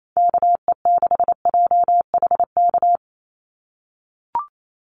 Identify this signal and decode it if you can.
Morse code, heard as a keyed tone, about 28 wpm, 700 Hz. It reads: KE6JHK
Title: USA Call Signs 28wpm